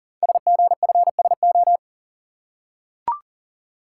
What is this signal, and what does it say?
Morse code, heard as a keyed tone, about 38 wpm, 700 Hz. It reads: SGFSO
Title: Code Groups - 5 Letters - 40wpm